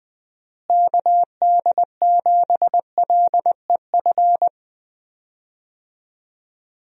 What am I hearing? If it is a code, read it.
Morse code, heard as a keyed tone, about 21 wpm, 700 Hz. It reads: KD7LEF
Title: USA Call Signs - 6 Characters - Repeated 2x - 20wpm